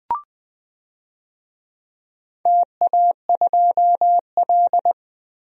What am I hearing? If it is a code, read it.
Morse code, heard as a keyed tone, about 21 wpm, 700 Hz. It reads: TA2L
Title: DXCC-100 Call Signs - 4 Characters - 20wpm